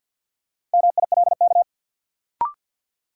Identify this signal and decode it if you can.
Morse code, heard as a keyed tone, about 47 wpm, 700 Hz. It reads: MILK
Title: Sentences from Top 1500 Words - New Word Drill - 50wpm